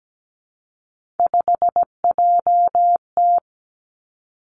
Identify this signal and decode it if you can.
Morse code, heard as a keyed tone, about 17 wpm, 700 Hz. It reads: 5JT
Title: Three Challenging Letter-Number Combinations 17wpm